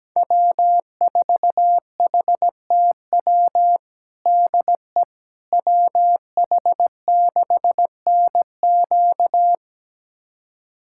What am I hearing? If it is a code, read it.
Morse code, heard as a keyed tone, about 17 wpm, 700 Hz. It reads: W4HTW DE WH6NQ
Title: USA Callsign DE Callsign - 5 Characters - 17wpm